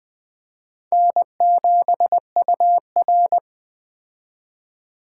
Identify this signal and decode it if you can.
Morse code, heard as a keyed tone, about 20 wpm, 700 Hz. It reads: N7UR
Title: Speed Racing - USA Call Signs 17wpm